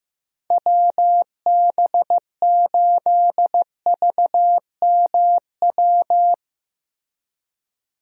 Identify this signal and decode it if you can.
Morse code, heard as a keyed tone, about 15 wpm, 700 Hz. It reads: WB8VMW
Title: USA Call Signs - 6 Characters - Repeated 2x - 15wpm